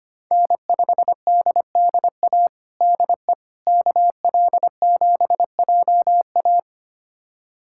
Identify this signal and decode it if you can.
Morse code, heard as a keyed tone, about 25 wpm, 700 Hz. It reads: N5DDA DE KL7JA
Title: USA Callsign DE Callsign - 5 Characters - 25wpm